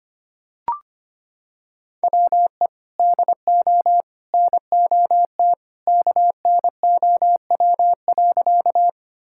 Morse code, heard as a keyed tone, about 25 wpm, 700 Hz. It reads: WE DO NOT KNOW.
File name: practice / 25wpm / Sentences with 3-Word Phrases - 25wpm